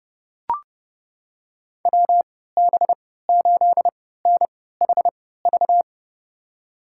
Morse code, heard as a keyed tone, about 30 wpm, spaced as for 17 wpm, 700 Hz. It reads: WB8NHV
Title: USA Call Signs - 3x Character Spacing - 30wpm